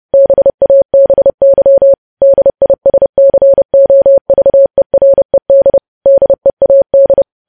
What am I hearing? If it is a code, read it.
Morse code, heard as a keyed tone, about 30 wpm, 550 Hz. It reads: BABY DISCOVERED DEAD
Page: news headlines in morse code at 30 wpm